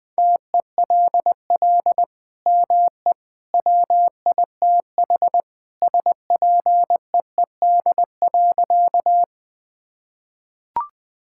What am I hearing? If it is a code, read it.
Morse code, heard as a keyed tone, about 20 wpm, 700 Hz. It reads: TELL ME WITH SPEED.
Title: Sentences from Top 1000 Words - Review Part 5 - 20wpm